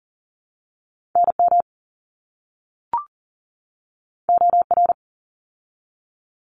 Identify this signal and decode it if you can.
Morse code, heard as a keyed tone, about 40 wpm, 700 Hz. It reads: NM OR
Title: ARRL Sections 40wpm